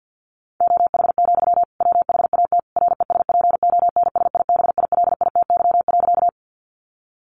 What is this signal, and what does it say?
Morse code, heard as a keyed tone, about 50 wpm, 700 Hz. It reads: OH, WHAT RESPONSIBILITY.
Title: Sentences from Top 1200 Words 50wpm